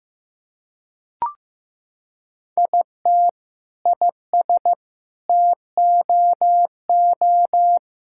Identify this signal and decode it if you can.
Morse code, heard as a keyed tone, about 16 wpm, 700 Hz. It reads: IT IS TOO
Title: Sentences from Top 1600 Words 15wpm